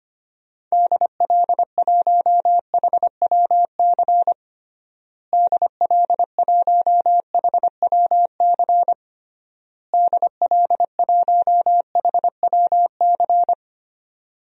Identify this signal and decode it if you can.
Morse code, heard as a keyed tone, about 25 wpm, 700 Hz. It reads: DL1HWC DL1HWC DL1HWC
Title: DXCC-100 Call Signs - 6 Characters - Repeated 3x - 25wpm